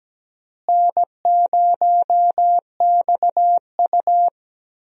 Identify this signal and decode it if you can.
Morse code, heard as a keyed tone, about 17 wpm, 700 Hz. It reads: N0XU
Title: USA Call Signs - 4 Characters - Repeated 2x - 17wpm